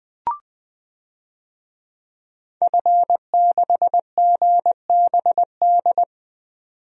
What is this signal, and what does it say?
Morse code, heard as a keyed tone, about 20 wpm, 700 Hz. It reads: F6GBD
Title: DXCC-100 Call Signs - 5 Characters - 20wpm